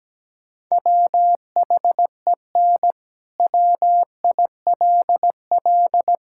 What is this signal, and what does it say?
Morse code, heard as a keyed tone, about 17 wpm, 700 Hz. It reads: WHEN WILL
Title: Sentences from Top 1000 Words - Review Part 6 - 17wpm